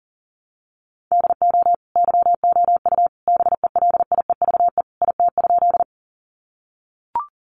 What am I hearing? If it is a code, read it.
Morse code, heard as a keyed tone, about 40 wpm, 700 Hz. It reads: DO YOU BELIEVE IT?
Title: Sentences from Top 1000 Words 40wpm